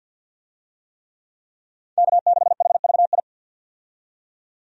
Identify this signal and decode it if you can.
Morse code, heard as a keyed tone, about 52 wpm, 700 Hz. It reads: K6HVI
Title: USA Call Signs - 5 Characters - 50wpm